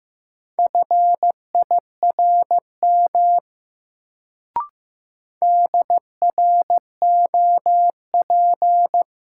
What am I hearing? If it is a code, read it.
Morse code, heard as a keyed tone, about 15 wpm, 700 Hz. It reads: FIRM DROP
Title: Sentences from Top 1400 Words - New Word Drill Alt - 15wpm